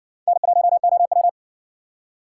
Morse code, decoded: N1OW

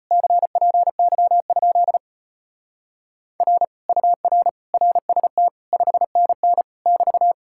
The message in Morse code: CPY? R UR RST 5NN <BT>

